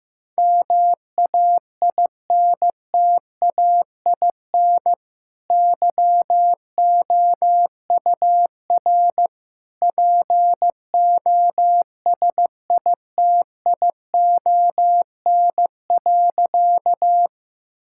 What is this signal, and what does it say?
Morse code, heard as a keyed tone, about 15 wpm, 700 Hz. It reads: MAINTAIN YOUR POSITION.
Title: Sentences from Top 1000 Words - Review Part 5 - 15wpm